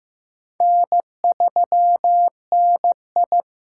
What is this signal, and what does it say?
Morse code, heard as a keyed tone, about 15 wpm, 700 Hz. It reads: N3NI
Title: USA Call Signs - 4 Characters - 8x Character Spacing - 15wpm